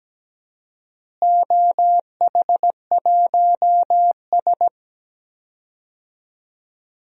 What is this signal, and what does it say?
Morse code, heard as a keyed tone, about 17 wpm, 700 Hz. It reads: OH1S
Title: DXCC-100 Call Signs - 4 Characters - Repeated 3x - 17wpm